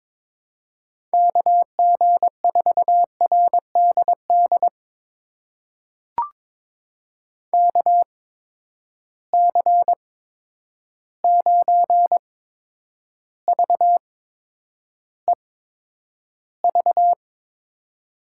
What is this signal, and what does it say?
Morse code, heard as a keyed tone, about 22 wpm, spaced as for 9 wpm, 700 Hz. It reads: KG4RDD KC9VEV
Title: USA Call Signs - 6 Characters - 8x Character Spacing - 22wpm